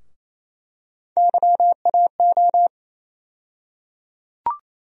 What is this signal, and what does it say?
Morse code, heard as a keyed tone, about 27 wpm, 700 Hz. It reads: YAO
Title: USA Call Sign Postfixes - 3 Letters 28wpm